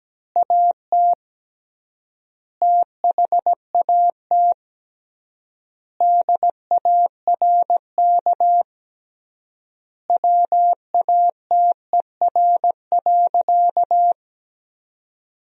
Morse code, heard as a keyed tone, about 17 wpm, 700 Hz. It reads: AT THAT DARK WATER.
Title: 3x Word-Spacing - Sentences from Top 500 Words - 17wpm